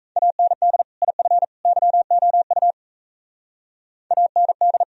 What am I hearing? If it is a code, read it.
Morse code, heard as a keyed tone, about 34 wpm, 700 Hz. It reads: AND IF YOU AND
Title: Speed-Racing - Sets of 3 Words 28wpm